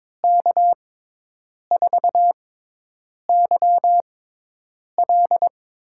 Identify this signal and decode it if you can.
Morse code, heard as a keyed tone, about 22 wpm, spaced as for 8 wpm, 700 Hz. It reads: K4YL
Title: USA Call Signs - 6 Characters - 6x Character Spacing - 22wpm